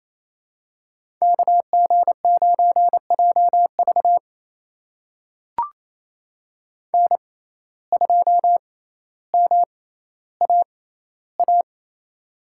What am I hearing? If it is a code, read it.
Morse code, heard as a keyed tone, about 28 wpm, spaced as for 14 wpm, 700 Hz. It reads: KG9JV N2MAA
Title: USA Call Signs - 6x Character Spacing - 28wpm